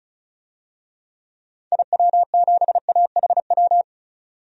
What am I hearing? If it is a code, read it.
Morse code, heard as a keyed tone, about 36 wpm, 700 Hz. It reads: IW7AHW
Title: DXCC-100 Call Signs - 6 Characters - 35wpm